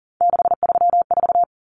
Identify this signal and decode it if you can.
Morse code, heard as a keyed tone, about 40 wpm, 700 Hz. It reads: 634
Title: Common Cut Numbers - 3-Digit 40wpm